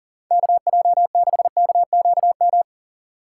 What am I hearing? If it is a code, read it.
Morse code, heard as a keyed tone, about 40 wpm, 700 Hz. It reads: KJ6KQM
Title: USA Call Signs - 6 Characters - 3x Character Spacing - 40wpm